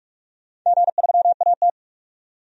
Morse code, heard as a keyed tone, about 43 wpm, 700 Hz. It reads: G3AT
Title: DXCC-100 Call Signs - 4 Characters - 45wpm